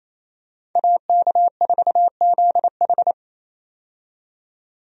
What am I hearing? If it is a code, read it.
Morse code, heard as a keyed tone, about 28 wpm, 700 Hz. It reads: AK4ZH
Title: USA Call Signs - 5 Characters - 28wpm